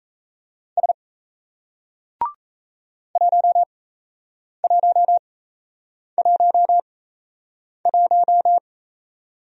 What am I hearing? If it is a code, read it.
Morse code, heard as a keyed tone, about 38 wpm, spaced as for 8 wpm, 700 Hz. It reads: S 1111